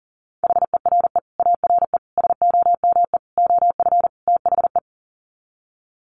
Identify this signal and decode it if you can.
Morse code, heard as a keyed tone, about 40 wpm, 700 Hz. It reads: HERE ARE SOME OF THE
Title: Sets of 5 Words Encore 40wpm